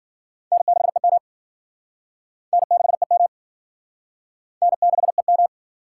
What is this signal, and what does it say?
Morse code, heard as a keyed tone, about 52 wpm, 700 Hz. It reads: N6EM N6EM N6EM